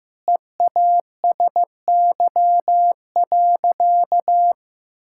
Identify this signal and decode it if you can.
Morse code, heard as a keyed tone, about 15 wpm, 700 Hz. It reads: EASY.
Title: Sentences from Top 1000 Words - Review Part 5 - 15wpm